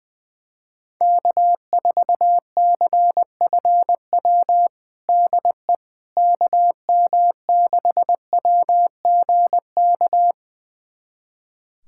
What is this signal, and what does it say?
Morse code, heard as a keyed tone, about 20 wpm, 700 Hz. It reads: K4CFW DE KM6WGK
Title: USA Callsign DE Callsign 20wpm